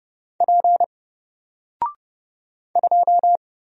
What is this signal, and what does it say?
Morse code, heard as a keyed tone, about 30 wpm, 700 Hz. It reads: P 2
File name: Single-Letter-Number-30wpm